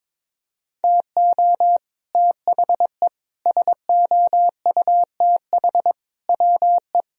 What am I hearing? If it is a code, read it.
Morse code, heard as a keyed tone, about 22 wpm, 700 Hz. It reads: TO THE SOUTH WE
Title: Sentences from Top 1000 Words - Review Part 2 - 22wpm